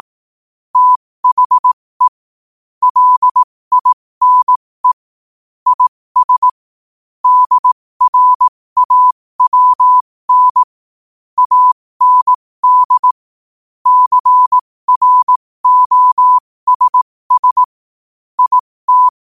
Morse code, heard as a keyed tone, about 18 wpm, spaced as for 15 wpm, 1000 Hz. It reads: THE LINE IS DRAWN AND CROSS IT
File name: qotdfiles / 15wpm